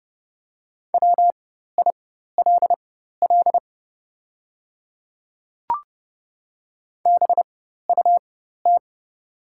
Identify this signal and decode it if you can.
Morse code, heard as a keyed tone, about 30 wpm, spaced as for 14 wpm, 700 Hz. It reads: WILL BUT
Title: 4x Character Spacing - Top 100 Words - 30wpm